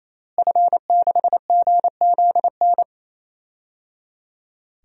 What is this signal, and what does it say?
Morse code, heard as a keyed tone, about 28 wpm, 700 Hz. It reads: F6GZN
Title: DXCC-100 Call Signs - 5 Characters - Repeated 3x - 28wpm